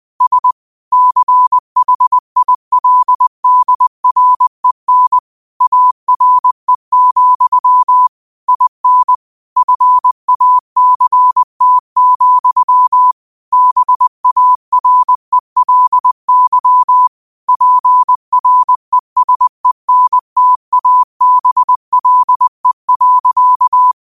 Morse code, heard as a keyed tone, about 20 wpm, 1000 Hz. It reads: S CHILDREN ARE, IN FACT, BARELY PRESENTABLE.
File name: qotdfiles / 20wpm